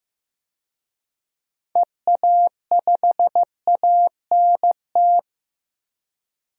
Morse code, heard as a keyed tone, about 15 wpm, 700 Hz. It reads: EA5ANT
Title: DXCC-100 Call Signs - 6 Characters - Repeated 2x - 15wpm